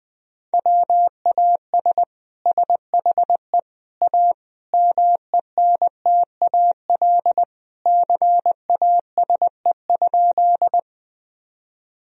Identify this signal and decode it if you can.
Morse code, heard as a keyed tone, about 21 wpm, 700 Hz. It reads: WAS SHE A MENTAL CASE?
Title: Sentences from Top 1000 Words - Review Part 4 - 20wpm